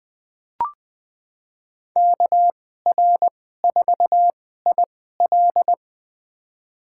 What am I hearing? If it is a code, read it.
Morse code, heard as a keyed tone, about 20 wpm, spaced as for 15 wpm, 700 Hz. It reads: KR4IL
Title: USA Call Signs - 5 Characters - 2x Character Spacing - 20wpm